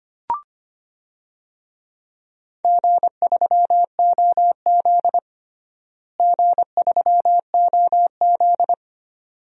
Morse code, heard as a keyed tone, about 25 wpm, 700 Hz. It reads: G3OZ G3OZ
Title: DXCC-100 Call Signs - 4 Characters - Repeated 2x - 25wpm